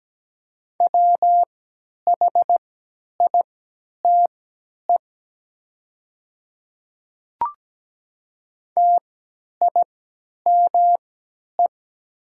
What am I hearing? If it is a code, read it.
Morse code, heard as a keyed tone, about 17 wpm, spaced as for 10 wpm, 700 Hz. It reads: WHITE TIME
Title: 3x Character Spacing - Top 300 Words - 17wpm